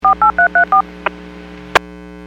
DTMF, heard as telephone touch tones, tones of about 110 ms. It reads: 48334